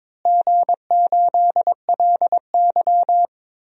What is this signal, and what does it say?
Morse code, heard as a keyed tone, about 22 wpm, 700 Hz. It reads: G8LY
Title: DXCC-100 Call Signs - 4 Characters - Repeated 3x - 22wpm